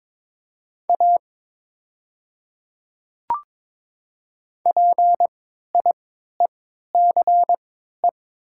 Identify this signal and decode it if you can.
Morse code, heard as a keyed tone, about 22 wpm, spaced as for 12 wpm, 700 Hz. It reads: A PIECE
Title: 3x Character Spacing - Top 500 Words - 22wpm